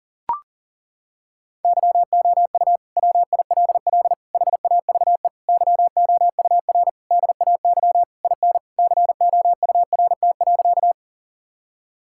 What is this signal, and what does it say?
Morse code, heard as a keyed tone, about 40 wpm, 700 Hz. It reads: YOU WILL HAVE YOUR DAY IN COURT.
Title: Sentences from Top 1600 Words 40wpm